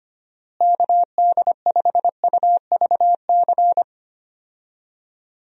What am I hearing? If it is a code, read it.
Morse code, heard as a keyed tone, about 25 wpm, 700 Hz. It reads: KD5UVC
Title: USA Call Signs - 6 Characters - Repeated 3x - 25wpm